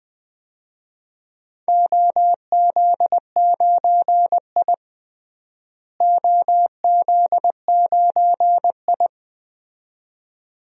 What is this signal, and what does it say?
Morse code, heard as a keyed tone, about 21 wpm, 700 Hz. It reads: OZ9I OZ9I
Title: DXCC-100 Call Signs - 4 Characters - Repeated 2x - 20wpm